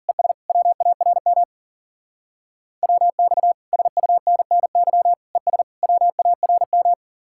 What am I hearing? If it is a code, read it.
Morse code, heard as a keyed tone, about 50 wpm, 700 Hz. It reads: ES WARM WX SUNNY ES WARM